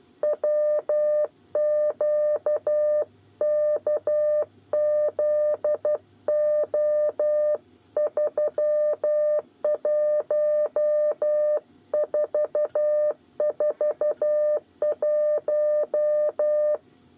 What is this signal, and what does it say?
Morse code, heard as a keyed tone, about 12 wpm, 600 Hz. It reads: WQKZO 31441
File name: 8 bis lezione cw 2